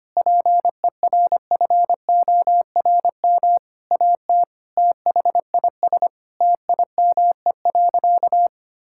Morse code, heard as a keyed tone, about 25 wpm, 700 Hz. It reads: PERFORM AT THIS TIME.
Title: Sentences from Top 1000 Words - Review Part 5 - 25wpm